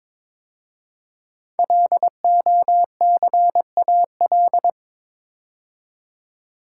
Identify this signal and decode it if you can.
Morse code, heard as a keyed tone, about 22 wpm, 700 Hz. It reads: LOCAL
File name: 5 Letter Words - First 3 Letters Given - 22wpm